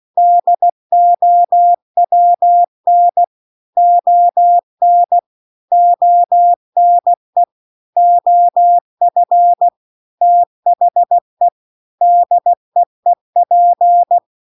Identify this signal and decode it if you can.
Morse code, heard as a keyed tone, about 16 wpm, 700 Hz. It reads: DOWN ON ONE OF THE DEEP
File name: CW-The-Big-Twoheaded-River-2-16WPM